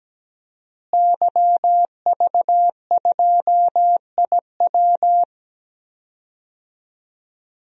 Morse code, heard as a keyed tone, about 17 wpm, 700 Hz. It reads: YV2IW
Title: DXCC-100 Call Signs - 5 Characters - Repeated 2x - 17wpm